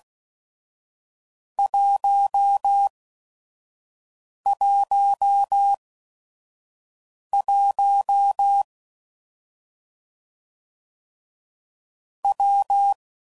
Morse code, heard as a keyed tone, about 16 wpm, spaced as for 5 wpm, 800 Hz. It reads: 111 W
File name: Combinations / Difficult Combos / WPJ WPJ1